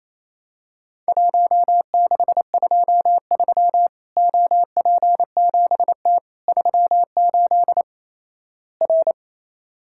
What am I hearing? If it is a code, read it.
Morse code, heard as a keyed tone, about 27 wpm, 700 Hz. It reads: 1623 OP7T 38 R
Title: Worked All Europe DX Contest - QTC Exchange - 28wpm